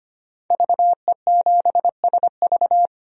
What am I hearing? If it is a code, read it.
Morse code, heard as a keyed tone, about 25 wpm, 700 Hz. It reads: VE7SV
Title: DXCC-100 Call Signs - 5 Characters - 25wpm